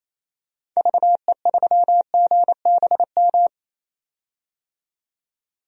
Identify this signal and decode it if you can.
Morse code, heard as a keyed tone, about 27 wpm, 700 Hz. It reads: VE3GBM
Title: DXCC-100 Call Signs - 6 Characters - 28wpm